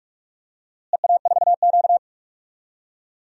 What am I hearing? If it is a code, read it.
Morse code, heard as a keyed tone, about 43 wpm, 700 Hz. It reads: EA4Q
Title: DXCC-100 Call Signs - 4 Characters - Repeated 2x - 45wpm